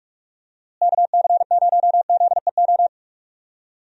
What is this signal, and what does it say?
Morse code, heard as a keyed tone, about 46 wpm, 700 Hz. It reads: KC0ZEO
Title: USA Call Signs - 6 Characters - Repeated 2x - 45wpm